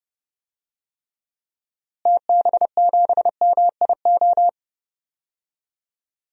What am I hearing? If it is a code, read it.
Morse code, heard as a keyed tone, about 30 wpm, 700 Hz. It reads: TB7MIO
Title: DXCC-100 Call Signs - 6 Characters - Repeated 2x - 30wpm